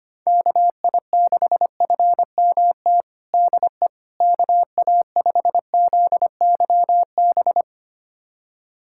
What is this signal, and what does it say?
Morse code, heard as a keyed tone, about 25 wpm, 700 Hz. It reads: KI6FMT DE KA5ZYB
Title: USA Callsign DE Callsign - 6 Characters - 25wpm